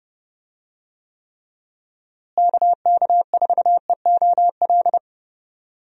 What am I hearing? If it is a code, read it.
Morse code, heard as a keyed tone, about 30 wpm, 700 Hz. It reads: KK4EOL
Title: USA Call Signs - 6 Characters - Repeated 3x - 30wpm